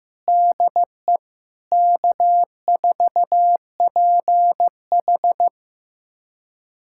Morse code, heard as a keyed tone, about 15 wpm, 700 Hz. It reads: DE K4PH
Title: USA Callsign DE Callsign 15wpm